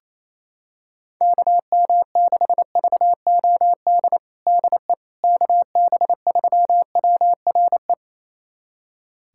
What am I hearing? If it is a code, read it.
Morse code, heard as a keyed tone, about 28 wpm, 700 Hz. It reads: KM6VOD DE KB3WRE
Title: USA Callsign DE Callsign 28wpm